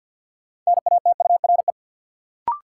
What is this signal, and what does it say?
Morse code, heard as a keyed tone, about 53 wpm, 700 Hz. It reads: NATURE
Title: Sentences from Top 600 Words - New Word Drill - 50wpm